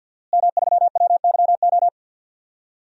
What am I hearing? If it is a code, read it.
Morse code, heard as a keyed tone, about 52 wpm, 700 Hz. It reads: M3WYO
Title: DXCC-100 Call Signs - 5 Characters - Repeated 3x - 50wpm